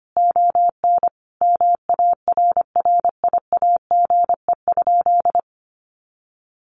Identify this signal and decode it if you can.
Morse code, heard as a keyed tone, about 25 wpm, 700 Hz. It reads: ON MARRIAGE?
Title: Sentences from Top 1200 Words 25wpm